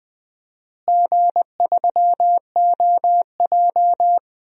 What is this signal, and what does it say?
Morse code, heard as a keyed tone, about 20 wpm, 700 Hz. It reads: G3OJ
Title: DXCC-100 Call Signs - 4 Characters - Repeated 2x - 20wpm